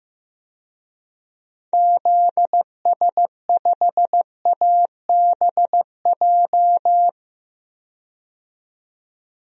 Morse code, heard as a keyed tone, about 15 wpm, 700 Hz. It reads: ZS5ABJ
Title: DXCC-100 Call Signs - 6 Characters - 15wpm